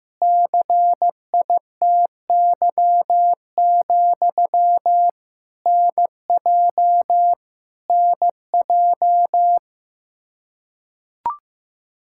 Morse code, heard as a keyed tone, about 15 wpm, 700 Hz. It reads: CITY, NJ NJ
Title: Practice QSO - Exchange 1 - Beginning - 15wpm